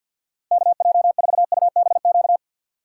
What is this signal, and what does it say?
Morse code, heard as a keyed tone, about 47 wpm, 700 Hz. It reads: KJ4UBQ
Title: USA Call Signs - 6 Characters - 8x Character Spacing - 50wpm